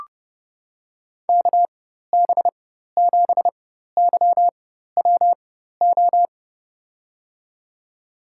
Morse code, heard as a keyed tone, about 30 wpm, spaced as for 14 wpm, 700 Hz. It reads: KB7YWO
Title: USA Call Signs - 4x Character Spacing - 30wpm